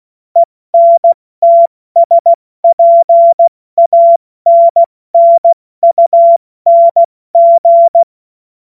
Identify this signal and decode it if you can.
Morse code, heard as a keyed tone, about 16 wpm, 680 Hz. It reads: ENTSPANNUNG
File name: Witz-16-September-2025-70-F80